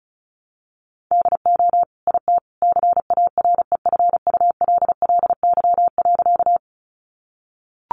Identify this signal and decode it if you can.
Morse code, heard as a keyed tone, about 34 wpm, 700 Hz. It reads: DO IT CAREFULLY.
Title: Sentences from Top 1300 Words 35wpm